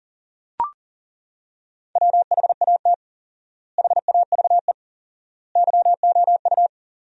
Morse code, heard as a keyed tone, about 37 wpm, 700 Hz. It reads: WHAT HAVE YOU
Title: 4x Word-Spacing - Sentences from Top 100 Words - 40wpm